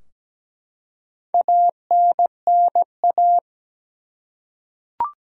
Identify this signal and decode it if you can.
Morse code, heard as a keyed tone, about 17 wpm, 700 Hz. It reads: ANNA
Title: USA Common Names - Max 4 Letters - 17wpm